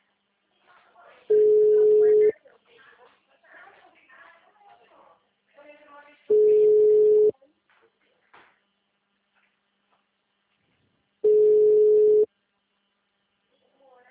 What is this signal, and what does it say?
Ringback tone (Europe, 425 Hz), with speech.
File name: out- -2022- - - .58222